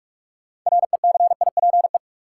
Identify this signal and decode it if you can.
Morse code, heard as a keyed tone, about 44 wpm, 700 Hz. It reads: RECIPE
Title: Sentences from Top 1200 Words - New Word Drill - 45wpm